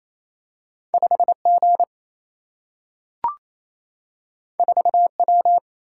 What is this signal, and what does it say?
Morse code, heard as a keyed tone, about 28 wpm, 700 Hz. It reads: 5G 4W
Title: Single-Letter Single-Number 28wpm